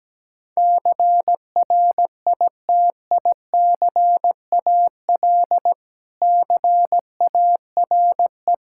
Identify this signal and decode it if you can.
Morse code, heard as a keyed tone, about 17 wpm, 700 Hz. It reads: CRITICAL CARE
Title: Practice QSO - Exchange 3 - More Info - 17wpm